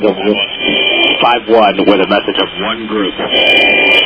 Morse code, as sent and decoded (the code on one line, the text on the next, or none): none